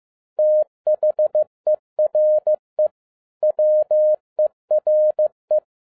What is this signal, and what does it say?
Morse code, heard as a keyed tone, about 15 wpm, 610 Hz. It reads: THERE WERE
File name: 2011-10-mp3 / n02-FIELD DAY CONTEST QRV FROM SAKAE VILLAGE 15wpm